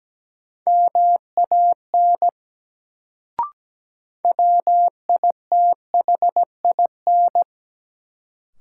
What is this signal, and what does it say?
Morse code, heard as a keyed tone, about 17 wpm, 700 Hz. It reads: MAN WITHIN
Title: Sets of 2 Words - New Word Drill - 17wpm